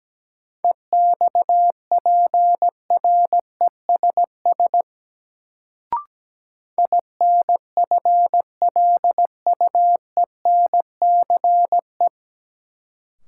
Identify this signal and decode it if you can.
Morse code, heard as a keyed tone, about 17 wpm, 700 Hz. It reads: EXPRESS INFLUENCE
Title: Sentences from Top 1100 Words - New Word Drill - 17wpm